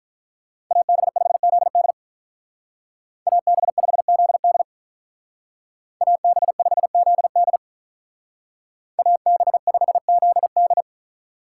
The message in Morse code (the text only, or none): AB5ZD AB5ZD AB5ZD AB5ZD